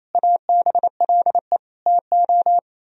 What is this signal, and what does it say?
Morse code, heard as a keyed tone, about 35 wpm, 700 Hz. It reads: ABLE TO
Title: Speed-Racing - Sets of 4 Words 28wpm